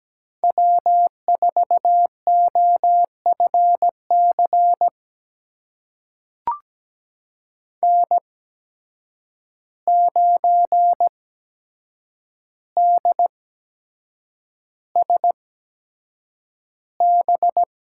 Morse code, heard as a keyed tone, about 17 wpm, spaced as for 7 wpm, 700 Hz. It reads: W4OFC N9DSB
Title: USA Call Signs - 5 Characters - 8x Character Spacing - 17wpm